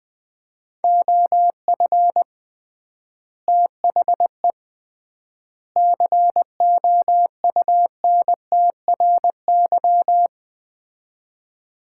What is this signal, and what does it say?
Morse code, heard as a keyed tone, about 21 wpm, 700 Hz. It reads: OF THE COUNTRY
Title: 3x Word-Spacing - Sets of 3 Words - 20wpm